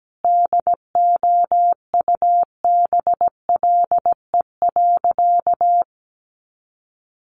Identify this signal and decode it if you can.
Morse code, heard as a keyed tone, about 17 wpm, 700 Hz. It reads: DOUBLE.
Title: Sentences from Top 1400 Words 17wpm